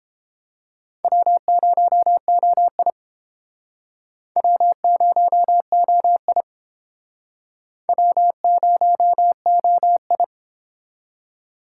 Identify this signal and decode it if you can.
Morse code, heard as a keyed tone, about 29 wpm, 700 Hz. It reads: W0OI W0OI W0OI